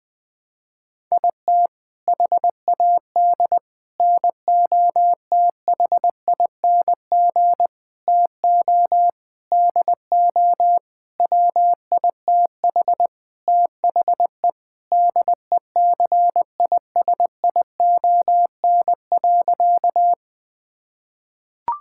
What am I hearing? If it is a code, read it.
Morse code, heard as a keyed tone, about 20 wpm, 700 Hz. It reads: IT HAD NOTHING TO DO WITH THE DECISION.
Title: Sentences with 6-Word Phrases - 20wpm